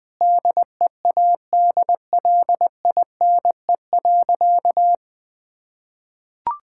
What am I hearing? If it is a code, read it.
Morse code, heard as a keyed tone, about 20 wpm, 700 Hz. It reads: DEADLINE.
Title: Sentences with 5-Word Phrases - Spoken Beginning - 20wpm